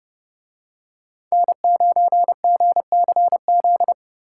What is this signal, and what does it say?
Morse code, heard as a keyed tone, about 31 wpm, 700 Hz. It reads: N9GCZ